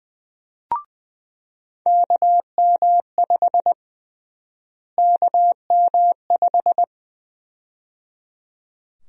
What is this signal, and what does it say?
Morse code, heard as a keyed tone, about 20 wpm, 700 Hz. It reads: KM5 KM5
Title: USA Call Sign Prefixes - Repeated 2x - 20wpm